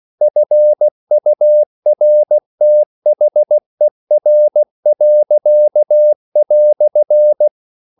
Morse code, heard as a keyed tone, about 16 wpm, 600 Hz. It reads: FURTHER."
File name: CW / Practice / EnglishQuotations0116